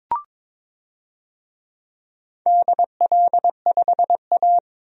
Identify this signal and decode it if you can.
Morse code, heard as a keyed tone, about 22 wpm, 700 Hz. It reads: DL5A
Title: DXCC-100 Call Signs - 4 Characters - 22wpm